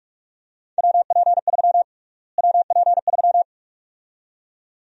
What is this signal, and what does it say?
Morse code, heard as a keyed tone, about 46 wpm, 700 Hz. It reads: WP3 WP3
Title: USA Call Sign Prefixes - Repeated 2x - 45wpm